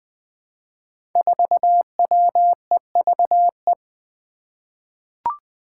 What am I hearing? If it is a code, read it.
Morse code, heard as a keyed tone, about 20 wpm, 700 Hz. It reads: 4WEVE
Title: Five Challenging Letter-Number Combinations 20wpm